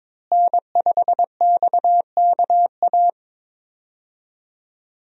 Morse code, heard as a keyed tone, about 21 wpm, 700 Hz. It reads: N5XKA
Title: USA Call Signs - 5 Characters - Repeated 2x - 22wpm